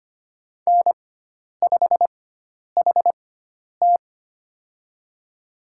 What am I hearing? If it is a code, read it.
Morse code, heard as a keyed tone, about 25 wpm, spaced as for 10 wpm, 700 Hz. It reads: N5HT
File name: Lesson 16g - Letter c - Call Signs 25wpm